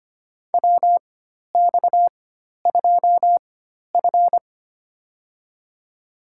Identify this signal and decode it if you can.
Morse code, heard as a keyed tone, about 25 wpm, spaced as for 12 wpm, 700 Hz. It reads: WX2F
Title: USA Call Signs - 4 Characters - 4x Character Spacing - 25wpm